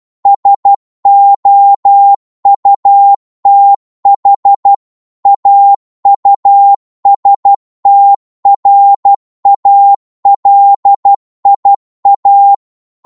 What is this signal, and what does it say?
Morse code, heard as a keyed tone, about 12 wpm, 800 Hz. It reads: SOUTH AUSTRALIA
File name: Morse / qso-12wpm